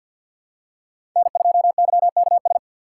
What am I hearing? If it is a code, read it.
Morse code, heard as a keyed tone, about 46 wpm, 700 Hz. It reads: N2YKS
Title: USA Call Signs - 4x Character Spacing - 50wpm